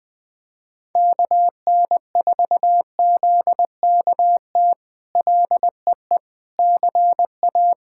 Morse code, heard as a keyed tone, about 20 wpm, 700 Hz. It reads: KN4ZKT LEE CA
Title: K1USN SST QSO Practice - 20wpm